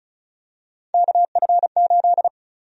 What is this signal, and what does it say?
Morse code, heard as a keyed tone, about 36 wpm, 700 Hz. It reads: KF8
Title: USA Call Sign Prefixes - Repeated 2x - 35wpm